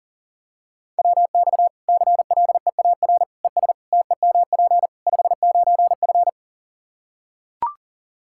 Morse code, heard as a keyed tone, about 40 wpm, 700 Hz. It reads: WX CLEAR ES TEMP 59F
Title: Practice QSO - Exchange 2 - Info - 40wpm